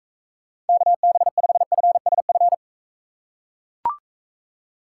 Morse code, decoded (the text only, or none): KB5FSF